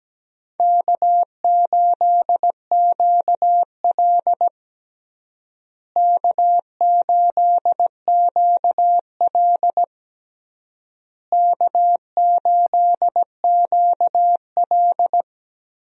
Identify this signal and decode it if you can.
Morse code, heard as a keyed tone, about 17 wpm, 700 Hz. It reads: K8QL K8QL K8QL
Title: USA Call Signs - 4 Characters - Repeated 3x - 17wpm